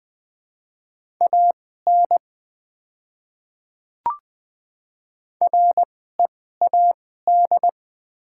Morse code, heard as a keyed tone, about 20 wpm, spaced as for 14 wpm, 700 Hz. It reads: AN READ
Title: 2x Character Spacing - Top 500 Words - 20wpm